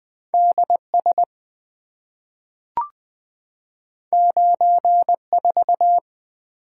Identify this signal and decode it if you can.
Morse code, heard as a keyed tone, about 20 wpm, 700 Hz. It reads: D S 9 4
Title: Two Letter-Number Combinations - 20wpm